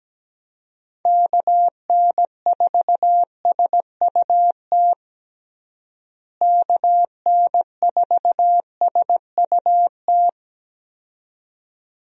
Morse code, heard as a keyed tone, about 17 wpm, 700 Hz. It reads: KN4SUT KN4SUT
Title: USA Call Signs - 6 Characters - Repeated 2x - 17wpm